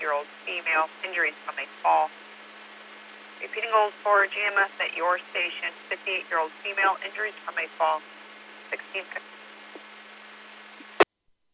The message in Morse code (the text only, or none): none